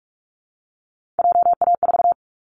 Morse code, heard as a keyed tone, about 46 wpm, 700 Hz. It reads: JA4
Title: Callsign Prefixes for Top 50 DXCC - 45wpm